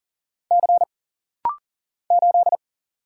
Morse code, heard as a keyed tone, about 39 wpm, 700 Hz. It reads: C 8
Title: Single Letter-Number - Rapid-Fire 40wpm